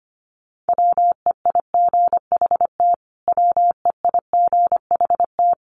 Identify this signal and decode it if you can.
Morse code, heard as a keyed tone, about 25 wpm, 700 Hz. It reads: WEIGHT WEIGHT
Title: Top 500 Words Repeated 25wpm